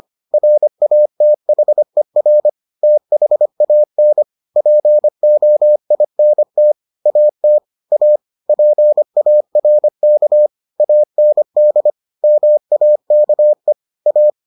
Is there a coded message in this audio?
Yes: Morse code from 0.3 to 14.3 s, heard as a keyed tone, about 25 wpm, 600 Hz. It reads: RATHER THAN POINT AT A PARK AND MAKE A